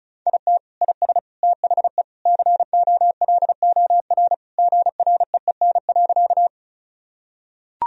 Morse code, heard as a keyed tone, about 34 wpm, 700 Hz. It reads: IT IS THE COLOR GREEN.
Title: Sentences from Top 1000 Words - Review Part 2 - 35wpm